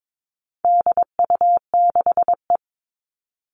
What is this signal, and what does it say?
Morse code, heard as a keyed tone, about 22 wpm, 700 Hz. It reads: DU6E
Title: All Cut Numbers - 4-Digit 22wpm